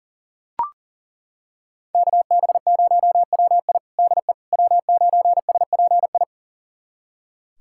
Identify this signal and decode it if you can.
Morse code, heard as a keyed tone, about 40 wpm, 690 Hz. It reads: KB0WI DE W9SPI
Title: USA Callsign DE Callsign - 5 Characters - 40wpm